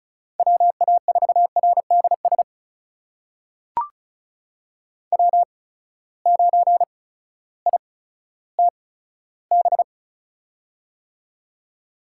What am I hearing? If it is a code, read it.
Morse code, heard as a keyed tone, about 36 wpm, spaced as for 16 wpm, 700 Hz. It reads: WA4RDS W9ITB
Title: USA Call Signs - 8x Character Spacing - 35wpm